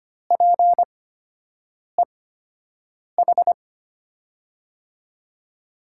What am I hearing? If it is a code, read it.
Morse code, heard as a keyed tone, about 25 wpm, spaced as for 7 wpm, 700 Hz. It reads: PEH
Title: USA Call Signs - 5 Characters - 8x Character Spacing - 25wpm